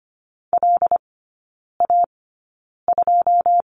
Morse code, heard as a keyed tone, about 25 wpm, spaced as for 9 wpm, 690 Hz. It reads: LA2
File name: Lesson 14e - Number 2 - Three Characters 25wpm